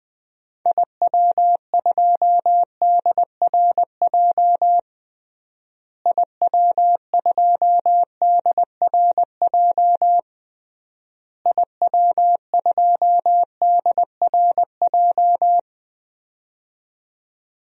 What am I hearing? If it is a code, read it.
Morse code, heard as a keyed tone, about 20 wpm, 700 Hz. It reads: IW2DRJ IW2DRJ IW2DRJ
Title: DXCC-100 Call Signs - 6 Characters - Repeated 3x - 20wpm